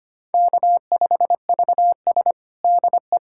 Morse code, heard as a keyed tone, about 25 wpm, 700 Hz. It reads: K5VS DE
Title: USA Callsign DE Callsign - 4 Characters - 25wpm